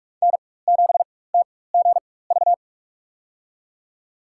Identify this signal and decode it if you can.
Morse code, heard as a keyed tone, about 43 wpm, spaced as for 21 wpm, 700 Hz. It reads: N7TGV
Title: USA Call Signs - 4x Character Spacing - 45wpm